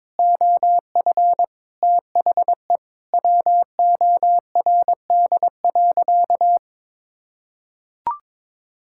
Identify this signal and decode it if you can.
Morse code, heard as a keyed tone, about 22 wpm, 700 Hz. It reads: OF THE WORD.
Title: Sentences from Top 1000 Words - Review Part 6 - 22wpm